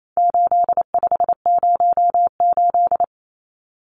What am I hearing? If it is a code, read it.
Morse code, heard as a keyed tone, about 28 wpm, 700 Hz. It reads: 8508
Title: Numbers - 4-Digit 28wpm